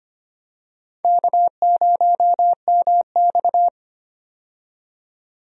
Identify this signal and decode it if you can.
Morse code, heard as a keyed tone, about 25 wpm, 700 Hz. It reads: K0MX
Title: USA Call Signs - 4 Characters - 25wpm